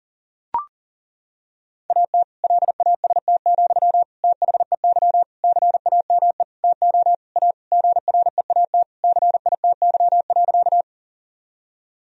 Morse code, heard as a keyed tone, about 40 wpm, 700 Hz. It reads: AT LAST, THEY CAME TO A GREAT CITY.
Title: Sentences from Top 1000 Words - Review Part 1 - 40wpm